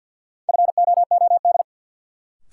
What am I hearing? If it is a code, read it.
Morse code, heard as a keyed tone, about 53 wpm, 700 Hz. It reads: FOOD
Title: Sets of 5 Words - New Word Drill - 50wpm